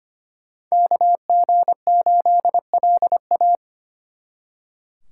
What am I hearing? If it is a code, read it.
Morse code, heard as a keyed tone, about 31 wpm, 700 Hz. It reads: KG8LA
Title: Speed Racing - USA Call Signs 25wpm